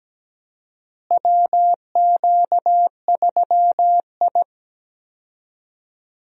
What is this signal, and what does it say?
Morse code, heard as a keyed tone, about 17 wpm, 700 Hz. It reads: WQ3I
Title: USA Call Signs - 4 Characters - Repeated 2x - 17wpm